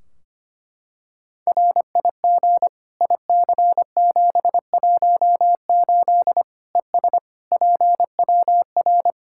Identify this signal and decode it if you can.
Morse code, heard as a keyed tone, about 25 wpm, 700 Hz. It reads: RIG IC718 ES PWR
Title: Practice QSO - Exchange 2 - Info - 25wpm